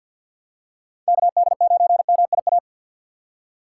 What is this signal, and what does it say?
Morse code, heard as a keyed tone, about 47 wpm, 700 Hz. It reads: KD9MIA
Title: USA Call Signs - 6 Characters - 4x Character Spacing - 50wpm